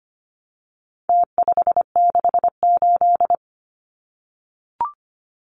Morse code, heard as a keyed tone, about 25 wpm, 700 Hz. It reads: T568
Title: Numbers - Contest Style - 4-Digit 25wpm